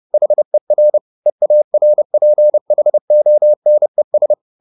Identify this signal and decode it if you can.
Morse code, heard as a keyed tone, about 30 wpm, 600 Hz. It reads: HER EARPHONES